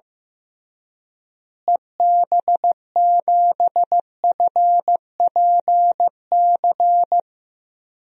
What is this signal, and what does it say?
Morse code, heard as a keyed tone, about 15 wpm, 700 Hz. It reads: EB7FPC
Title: DXCC-100 Call Signs - 6 Characters - Repeated 3x - 15wpm